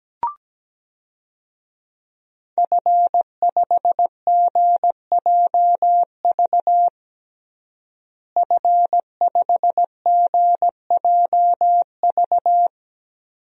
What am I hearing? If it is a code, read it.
Morse code, heard as a keyed tone, about 17 wpm, 700 Hz. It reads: F5GJV F5GJV
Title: DXCC-100 Call Signs - 5 Characters - Repeated 3x - 17wpm